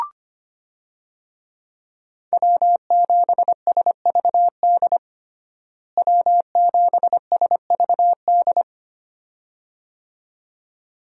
Morse code, heard as a keyed tone, about 25 wpm, 700 Hz. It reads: W7SVD W7SVD
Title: USA Call Signs - 5 Characters - Repeated 2x - 25wpm